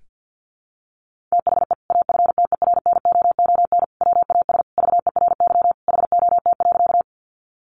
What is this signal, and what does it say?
Morse code, heard as a keyed tone, about 50 wpm, 700 Hz. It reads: THE AFTERNOON WAS VERY HOT.
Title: Sentences from Top 1500 Words 50wpm